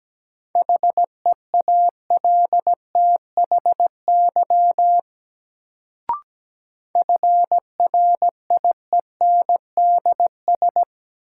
Morse code, heard as a keyed tone, about 17 wpm, 700 Hz. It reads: HEALTHY FRIENDS
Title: Sentences from Top 1000 Words - Review Part 5 - New Word Drill - 17wpm